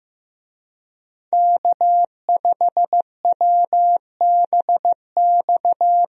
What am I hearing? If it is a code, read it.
Morse code, heard as a keyed tone, about 15 wpm, 700 Hz. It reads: K5WBX
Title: USA Call Signs - 5 Characters - Repeated 2x - 15wpm